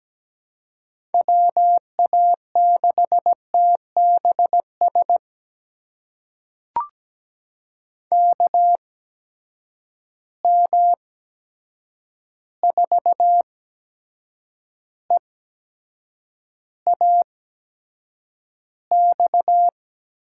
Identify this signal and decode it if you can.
Morse code, heard as a keyed tone, about 17 wpm, spaced as for 7 wpm, 700 Hz. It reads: WA6TBS KM4EAX
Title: USA Call Signs - 6 Characters - 8x Character Spacing - 17wpm